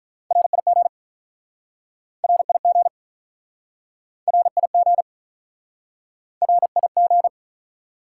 Morse code, heard as a keyed tone, about 48 wpm, 700 Hz. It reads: RIG RIG RIG RIG